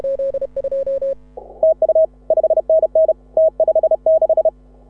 Morse code, heard as a keyed tone, about 31 wpm, 640 Hz. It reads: TU 5NN T56
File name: AUDIO2018 / HA6OI CW